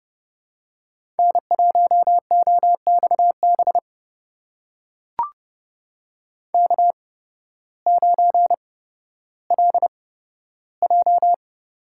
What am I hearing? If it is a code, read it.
Morse code, heard as a keyed tone, about 30 wpm, spaced as for 14 wpm, 700 Hz. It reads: N1OXB K9LJ
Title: USA Call Signs - 5 Characters - 8x Character Spacing - 30wpm